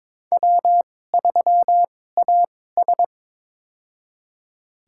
Morse code, heard as a keyed tone, about 23 wpm, spaced as for 16 wpm, 700 Hz. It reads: W3AS
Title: USA Call Signs - 4 Characters - 2x Character Spacing - 22wpm